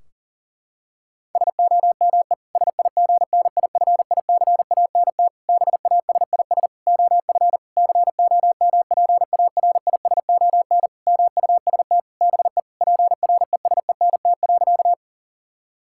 Morse code, heard as a keyed tone, about 37 wpm, 700 Hz. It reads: SOME SIGNIFICANT BASIS OF COMPARISON MUST BE PRESENT.
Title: Sentences from Top 1600 Words 40wpm